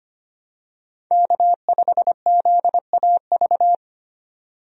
Morse code, heard as a keyed tone, about 25 wpm, 700 Hz. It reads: K5ZAV
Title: USA Call Signs - 5 Characters - 25wpm